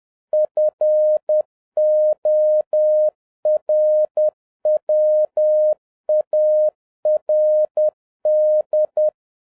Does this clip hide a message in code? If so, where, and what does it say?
Yes: Morse code from 0.3 to 9.1 s, heard as a keyed tone, about 10 wpm, 610 Hz. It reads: FORWARD